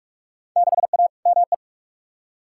Morse code, heard as a keyed tone, about 46 wpm, 700 Hz. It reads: 6A ME
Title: Field Day Exchanges - 45wpm